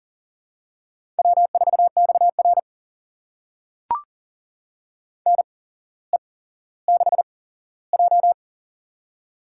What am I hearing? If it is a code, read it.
Morse code, heard as a keyed tone, about 39 wpm, spaced as for 17 wpm, 700 Hz. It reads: W4XR NE6J
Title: USA Call Signs - 4 Characters - 8x Character Spacing - 40wpm